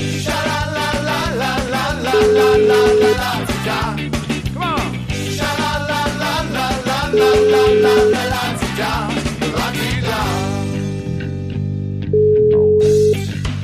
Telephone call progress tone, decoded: ringback tone

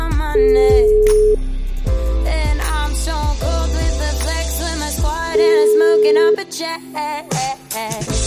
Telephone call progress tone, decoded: ringback tone

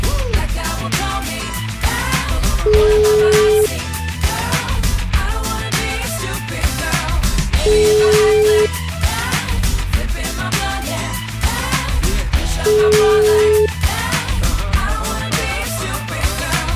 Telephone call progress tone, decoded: ringback tone

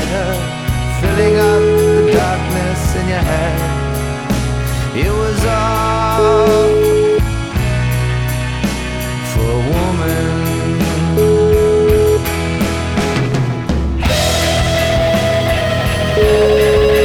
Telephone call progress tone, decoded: ringback tone

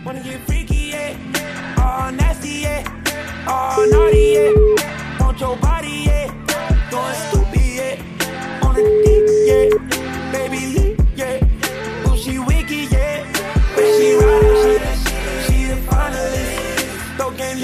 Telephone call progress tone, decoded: ringback tone

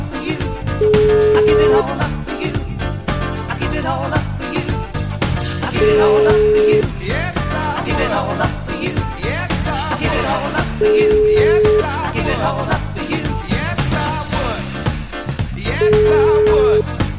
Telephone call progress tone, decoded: ringback tone

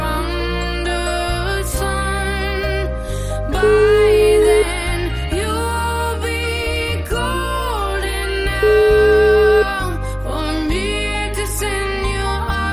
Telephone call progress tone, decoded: ringback tone

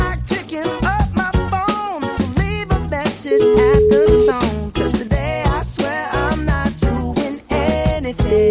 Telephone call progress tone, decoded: ringback tone

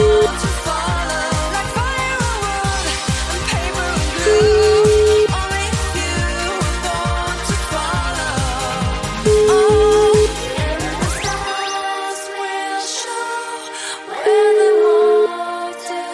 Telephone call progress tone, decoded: ringback tone

